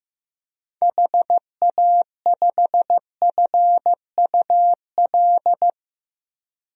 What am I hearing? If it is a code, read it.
Morse code, heard as a keyed tone, about 15 wpm, 700 Hz. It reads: HA5FUL
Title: DXCC-100 Call Signs - 6 Characters - 15wpm